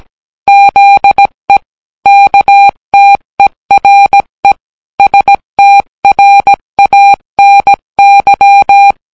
Morse code, heard as a keyed tone, about 17 wpm, 800 Hz. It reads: ZE KTERE STRANY